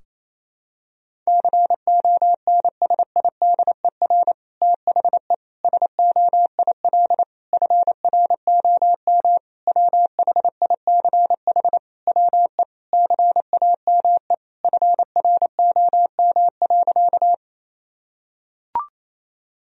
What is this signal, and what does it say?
Morse code, heard as a keyed tone, about 28 wpm, 700 Hz. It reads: CONSIDER THE SOIL FROM WHICH WE CAME FROM.
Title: Sentences from Top 1000 Words - Review Part 4 - 28wpm